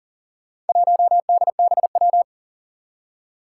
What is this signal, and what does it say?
Morse code, heard as a keyed tone, about 39 wpm, 700 Hz. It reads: 1DBW
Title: Four Challenging Letter-Number Combinations 40wpm